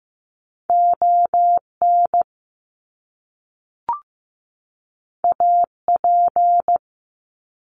Morse code, heard as a keyed tone, about 15 wpm, 700 Hz. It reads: ON AP
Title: Top Two Letter Combinations in Words 15wpm